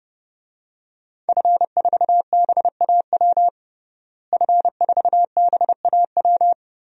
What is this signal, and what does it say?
Morse code, heard as a keyed tone, about 32 wpm, 700 Hz. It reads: F4BAW F4BAW
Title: DXCC-100 Call Signs - 5 Characters - Repeated 3x - 30wpm